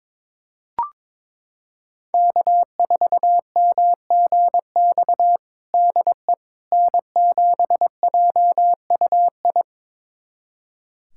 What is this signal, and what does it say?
Morse code, heard as a keyed tone, about 22 wpm, 700 Hz. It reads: K4MGX DE N7JUI
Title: USA Callsign DE Callsign - 5 Characters - 22wpm